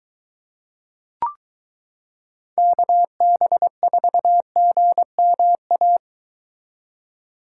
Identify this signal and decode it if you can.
Morse code, heard as a keyed tone, about 22 wpm, 700 Hz. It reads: KB4GMA